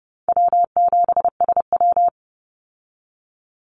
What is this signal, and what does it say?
Morse code, heard as a keyed tone, about 30 wpm, 700 Hz. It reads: W7SW
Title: Call Signs Repeated 2x - 30wpm